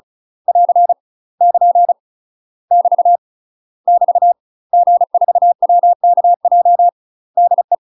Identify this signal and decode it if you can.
Morse code, heard as a keyed tone, about 34 wpm, 700 Hz. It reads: <AR> <KN> <BT> <BT> G4WKJ DE
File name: UK MASTER QSOs TRAINING / 35WPM / UK MASTER QSOs 0021